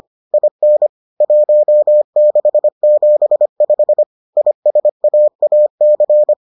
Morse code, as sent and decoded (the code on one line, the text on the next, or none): .. -. / .---- -.... --... ..... / .. ... .- .- -.-.
IN 1675 ISAAC